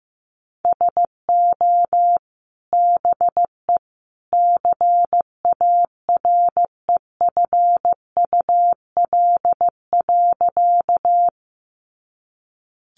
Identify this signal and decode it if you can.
Morse code, heard as a keyed tone, about 15 wpm, 700 Hz. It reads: SO BE CAREFUL.
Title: A Sound Of Thunder 15wpm